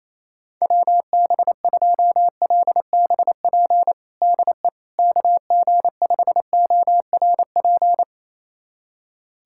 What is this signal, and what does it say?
Morse code, heard as a keyed tone, about 28 wpm, 700 Hz. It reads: WB2LBP DE KG5ORP
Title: USA Callsign DE Callsign - 6 Characters - 28wpm